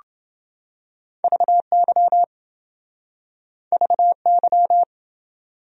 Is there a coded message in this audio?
Yes: Morse code from 1.2 to 4.8 s, heard as a keyed tone, about 28 wpm, 700 Hz. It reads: VY VY